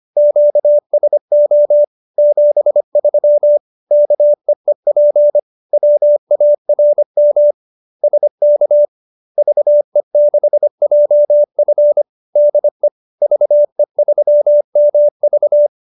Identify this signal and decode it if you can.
Morse code, heard as a keyed tone, about 25 wpm, 590 Hz. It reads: QSO 73 KEEP WARM SK VE6JF DE VE3MV